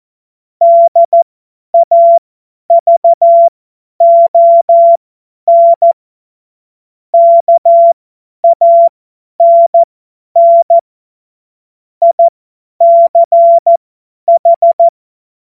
Morse code, heard as a keyed tone, about 14 wpm, spaced as for 10 wpm, 680 Hz. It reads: DAVON KANN ICH
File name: Morsekurs / Witze / Witz 17. Januar 2023 50 F70